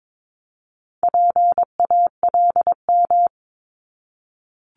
Morse code, heard as a keyed tone, about 22 wpm, 700 Hz. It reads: PALM
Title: Multi-letter Phonemes 22wpm